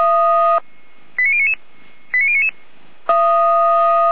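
Five-tone selective call (ZVEI), heard as quick runs of five tones, tones of about 70 ms. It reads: 89099 89099
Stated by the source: ZVEI five-tone selective call received on an SDR.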